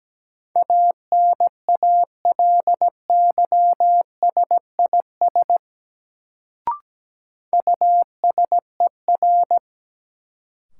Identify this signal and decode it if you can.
Morse code, heard as a keyed tone, about 17 wpm, 700 Hz. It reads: ANALYSIS USER
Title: Sentences from Top 1000 Words - Review Part 5 - New Word Drill - 17wpm